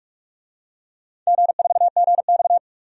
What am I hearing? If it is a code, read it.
Morse code, heard as a keyed tone, about 46 wpm, 700 Hz. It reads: G4GX
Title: DXCC-100 Call Signs - 4 Characters - 45wpm